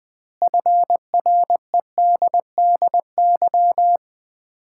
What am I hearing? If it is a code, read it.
Morse code, heard as a keyed tone, about 21 wpm, 700 Hz. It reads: FREDDY
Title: Common QSO Phrases - 2x Word-Spacing - 20wpm